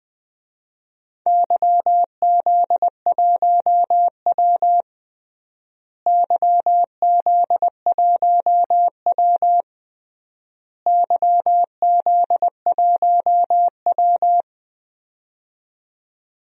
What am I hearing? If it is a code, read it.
Morse code, heard as a keyed tone, about 20 wpm, 700 Hz. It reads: YZ1W YZ1W YZ1W
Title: DXCC-100 Call Signs - 4 Characters - Repeated 3x - 20wpm